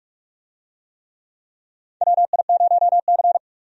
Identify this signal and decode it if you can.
Morse code, heard as a keyed tone, about 41 wpm, 700 Hz. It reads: WI0C
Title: USA Call Signs - 4 Characters - 45wpm